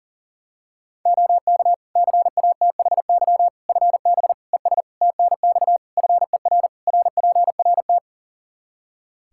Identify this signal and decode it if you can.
Morse code, heard as a keyed tone, about 40 wpm, 700 Hz. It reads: OK CATHY FB ES TNX FER RPRT
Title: Practice QSO - Exchange 2 - Info - 40wpm